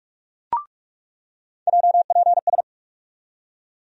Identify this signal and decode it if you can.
Morse code, heard as a keyed tone, about 44 wpm, 700 Hz. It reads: JPS
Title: USA Call Sign Postfixes - 3 Letters 45wpm